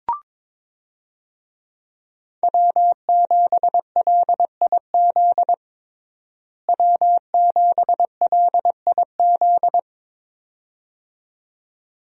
Morse code, heard as a keyed tone, about 22 wpm, 700 Hz. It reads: W7LIZ W7LIZ
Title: USA Call Signs - 5 Characters - Repeated 2x - 22wpm